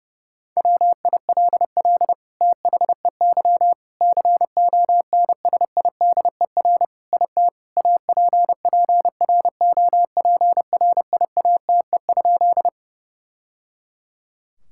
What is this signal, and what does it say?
Morse code, heard as a keyed tone, about 30 wpm, 700 Hz. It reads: WILL THEY CONSIDER IT APPROPRIATE?
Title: Sentences from Top 1000 Words - Review Part 5 - 30wpm